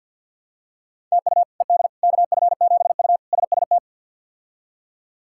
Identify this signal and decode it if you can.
Morse code, heard as a keyed tone, about 52 wpm, 690 Hz. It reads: TU ED KF7U SST
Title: K1USN SST QSO Practice - 50wpm